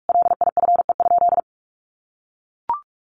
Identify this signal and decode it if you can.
Morse code, heard as a keyed tone, about 44 wpm, 700 Hz. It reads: LIFE?
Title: Sentences from Top 1000 Words 45wpm